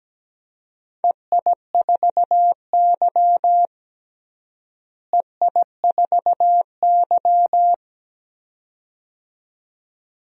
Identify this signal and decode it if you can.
Morse code, heard as a keyed tone, about 17 wpm, 700 Hz. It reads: EI4Y EI4Y
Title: DXCC-100 Call Signs - 4 Characters - Repeated 2x - 17wpm